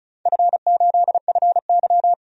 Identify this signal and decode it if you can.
Morse code, heard as a keyed tone, about 36 wpm, 690 Hz. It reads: F8FY
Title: DXCC-100 Call Signs - 4 Characters - 35wpm